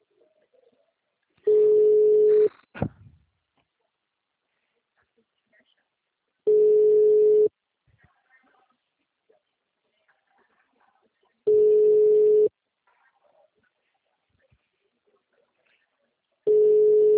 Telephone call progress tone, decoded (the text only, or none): ringback tone